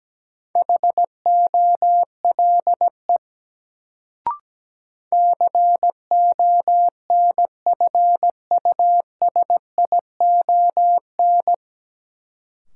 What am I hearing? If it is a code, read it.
Morse code, heard as a keyed tone, about 17 wpm, 700 Hz. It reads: HOLE CONFUSION
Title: Sentences from Top 1600 Words - New Word Drill - 17wpm